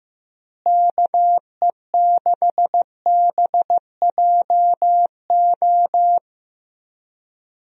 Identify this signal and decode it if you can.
Morse code, heard as a keyed tone, about 14 wpm, 700 Hz. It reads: KE6BJO
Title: USA Call Signs - 6 Characters - Repeated 3x - 15wpm